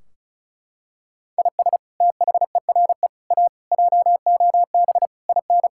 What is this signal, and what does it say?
Morse code, heard as a keyed tone, about 34 wpm, 700 Hz. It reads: IS THERE A JOB IN
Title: Sentences from Top 1600 Words 35wpm